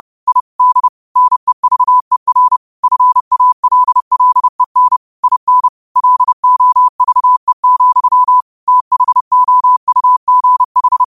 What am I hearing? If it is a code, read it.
Morse code, heard as a keyed tone, about 30 wpm, 1000 Hz. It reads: I D NEVER FALLEN IN LOVE, THOUGH